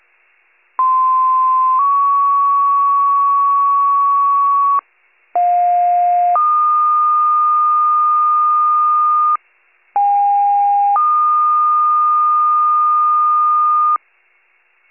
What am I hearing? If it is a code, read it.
Two-tone page, heard as a one-second tone, then a three-second tone, about 1040 then 1120 Hz, about 710 then 1220 Hz, about 810 then 1220 Hz. It is heard from 0.8 to 14.0 s.